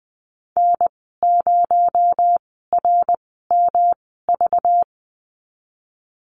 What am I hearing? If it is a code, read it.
Morse code, heard as a keyed tone, about 20 wpm, 700 Hz. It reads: N 0 R M V
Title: 231 Lesson 37g - Letter x - Call Signs 20wpm